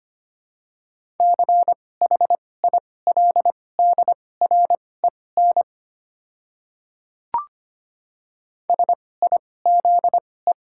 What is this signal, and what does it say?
Morse code, heard as a keyed tone, about 25 wpm, spaced as for 18 wpm, 700 Hz. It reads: CHILDREN SIZE
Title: 2x Character Spacing - Top 500 Words - 25wpm